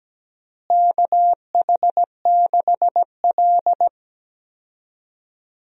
Morse code, heard as a keyed tone, about 17 wpm, 700 Hz. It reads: KH6L
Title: DXCC-100 Call Signs - 4 Characters - Repeated 3x - 17wpm